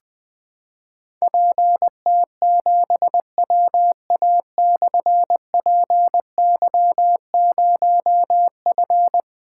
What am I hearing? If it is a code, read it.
Morse code, heard as a keyed tone, about 20 wpm, 700 Hz. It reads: PT7WA/PY0F
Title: Difficult Call Signs 20wpm